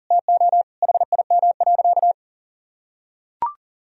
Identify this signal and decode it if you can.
Morse code, heard as a keyed tone, about 37 wpm, 700 Hz. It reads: TO HIM.
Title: Sentences from Top 1000 Words - Review Part 3 - 40wpm